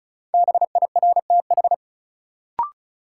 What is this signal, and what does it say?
Morse code, heard as a keyed tone, about 36 wpm, 700 Hz. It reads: BIRTH
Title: Sentences from Top 800 Words - New Word Drill - 35wpm